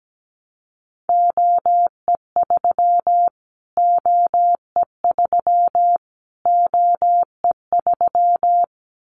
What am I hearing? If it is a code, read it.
Morse code, heard as a keyed tone, about 17 wpm, 700 Hz. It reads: OE3 OE3 OE3
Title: Callsign Prefixes for Top 25 DXCC - Repeated 3x - 17wpm